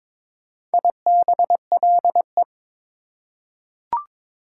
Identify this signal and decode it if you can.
Morse code, heard as a keyed tone, about 22 wpm, 700 Hz. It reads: IBLE
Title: Prefixes and Suffixes 22wpm